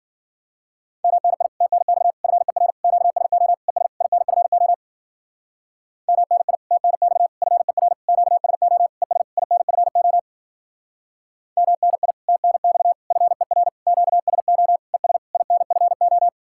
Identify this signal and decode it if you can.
Morse code, heard as a keyed tone, about 50 wpm, 700 Hz. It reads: MNI TNX FER QSO ES INFO MNI TNX FER QSO ES INFO MNI TNX FER QSO ES INFO